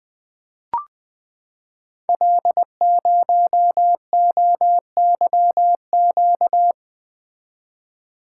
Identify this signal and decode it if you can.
Morse code, heard as a keyed tone, about 20 wpm, 700 Hz. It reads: L0OYQ
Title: Code Groups - 5 Letter:Numbers - 20wpm